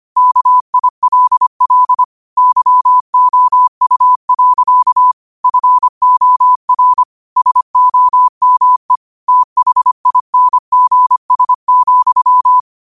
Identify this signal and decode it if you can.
Morse code, heard as a keyed tone, about 25 wpm, 1000 Hz. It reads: KILL YOU. FOR SOME THINGS,
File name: qotdfiles / 25wpm